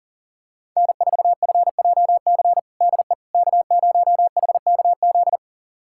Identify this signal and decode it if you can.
Morse code, heard as a keyed tone, about 40 wpm, 700 Hz. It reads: N4FJC DE K0HKZ
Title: USA Callsign DE Callsign - 5 Characters - 40wpm